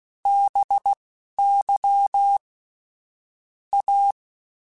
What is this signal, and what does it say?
Morse code, heard as a keyed tone, about 16 wpm, spaced as for 11 wpm, 800 Hz. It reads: BY A
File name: Technical Subjects / Technical Subjects-mp3s / XL @10